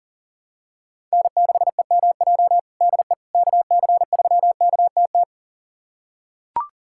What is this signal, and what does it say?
Morse code, heard as a keyed tone, about 40 wpm, 700 Hz. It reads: N6EMJ DE KC3KTT
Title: USA Callsign DE Callsign 40wpm